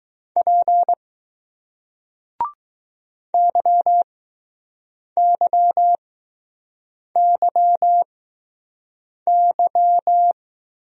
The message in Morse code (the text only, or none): P YYYY